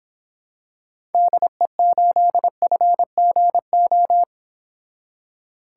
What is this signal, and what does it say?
Morse code, heard as a keyed tone, about 26 wpm, 700 Hz. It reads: DE8FGO